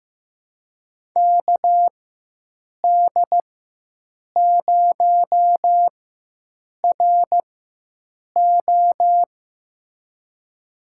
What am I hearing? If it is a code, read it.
Morse code, heard as a keyed tone, about 15 wpm, spaced as for 7 wpm, 700 Hz. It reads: KD0RO
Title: USA Call Signs - 4x Character Spacing - 15wpm